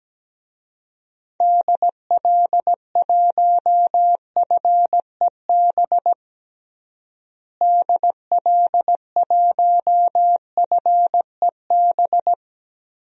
Morse code, heard as a keyed tone, about 17 wpm, 700 Hz. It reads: DL1FEB DL1FEB
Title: DXCC-100 Call Signs - 6 Characters - Repeated 3x - 17wpm